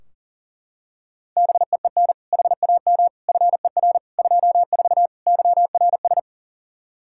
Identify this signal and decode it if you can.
Morse code, heard as a keyed tone, about 37 wpm, 700 Hz. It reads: BEEN HAM FER 24 YRS
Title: Practice QSO - Exchange 3 - More Info - 40wpm